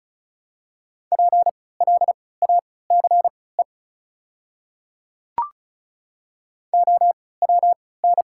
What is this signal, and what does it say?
Morse code, heard as a keyed tone, about 34 wpm, spaced as for 20 wpm, 700 Hz. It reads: PLACE OWN
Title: 3x Character Spacing - Top 300 Words - 35wpm